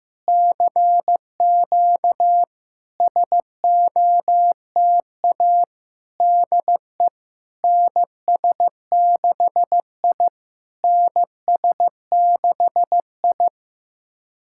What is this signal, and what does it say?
Morse code, heard as a keyed tone, about 15 wpm, 700 Hz. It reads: CQ SOTA DE NS6I NS6I
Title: SOTA QSO Practice 15wpm